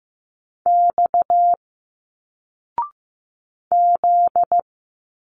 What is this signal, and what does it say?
Morse code, heard as a keyed tone, about 15 wpm, 700 Hz. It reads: X Z
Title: Single Letters 15wpm